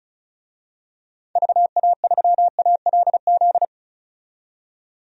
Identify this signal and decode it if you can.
Morse code, heard as a keyed tone, about 35 wpm, 700 Hz. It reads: VA3ALZ
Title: DXCC-100 Call Signs - 6 Characters - 35wpm